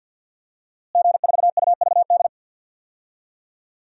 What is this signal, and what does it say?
Morse code, heard as a keyed tone, about 52 wpm, 700 Hz. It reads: G4UVD
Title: DXCC-100 Call Signs - 5 Characters - Repeated 2x - 50wpm